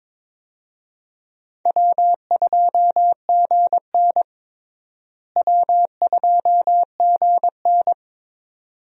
Morse code, heard as a keyed tone, about 22 wpm, 700 Hz. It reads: W2GN W2GN
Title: USA Call Signs - 4 Characters - Repeated 2x - 22wpm